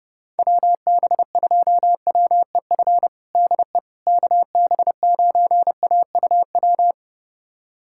Morse code, heard as a keyed tone, about 30 wpm, 700 Hz. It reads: WB2WEF DE KB9AUW
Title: USA Callsign DE Callsign 30wpm